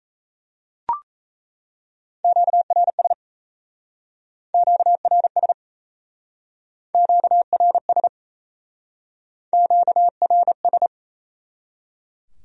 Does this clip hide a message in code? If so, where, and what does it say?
Yes: Morse code from 2.2 to 10.9 s, heard as a keyed tone, about 38 wpm, 700 Hz. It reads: QRS QRS QRS QRS